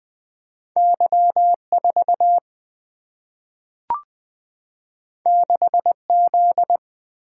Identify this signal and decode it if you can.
Morse code, heard as a keyed tone, about 20 wpm, 700 Hz. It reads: Y4 6Z
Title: Single-Letter Single-Number 20wpm